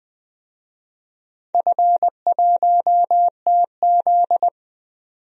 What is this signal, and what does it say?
Morse code, heard as a keyed tone, about 20 wpm, 700 Hz. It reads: F1TZ
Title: DXCC-100 Call Signs - 4 Characters - 20wpm